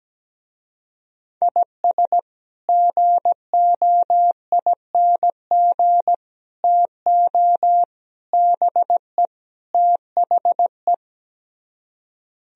Morse code, heard as a keyed tone, about 17 wpm, 700 Hz. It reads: IS GOING TO BE THE
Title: Speed-Racing - Sets of 5 Words 17wpm